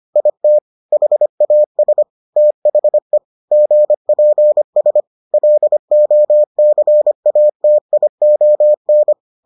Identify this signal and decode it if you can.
Morse code, heard as a keyed tone, about 25 wpm, 600 Hz. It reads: IT HAS THE GPS LOCATION